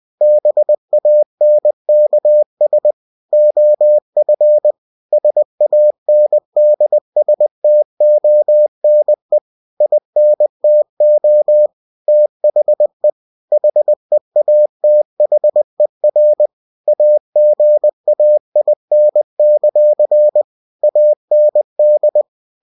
Morse code, heard as a keyed tone, about 20 wpm, 600 Hz. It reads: BANKS OF SANDSTONE INTO THE HEATHER AGAIN; AND
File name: cw-books / winnie-the-pooh / winnie-20wpm-page-0028